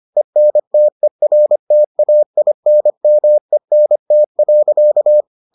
Morse code, heard as a keyed tone, about 25 wpm, 600 Hz. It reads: ENTERTAINMENT.